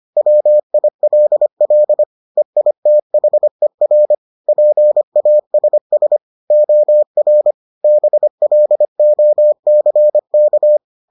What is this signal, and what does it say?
Morse code, heard as a keyed tone, about 25 wpm, 600 Hz. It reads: WILL EITHER PASS OR BLOCK